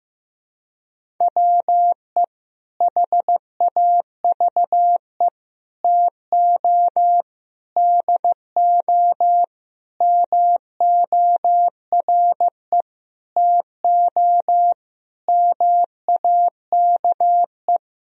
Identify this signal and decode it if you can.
Morse code, heard as a keyed tone, about 15 wpm, 700 Hz. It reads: WE HAVE TO DO MORE TO MAKE
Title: Sentences from Top 1000 Words - Review Part 3 - 15wpm